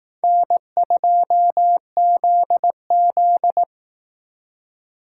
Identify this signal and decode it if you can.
Morse code, heard as a keyed tone, about 18 wpm, 700 Hz. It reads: N2ZZ
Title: Speed Racing - USA 1x2 Call Signs 15wpm